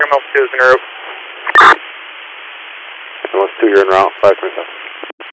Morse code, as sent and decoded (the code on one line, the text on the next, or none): none